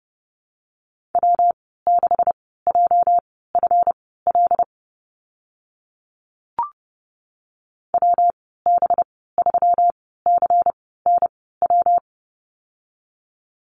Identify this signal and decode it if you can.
Morse code, heard as a keyed tone, about 30 wpm, spaced as for 17 wpm, 700 Hz. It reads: W6JFL WB3CNW
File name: Lesson 37g - Letter x - Call Signs 30wpm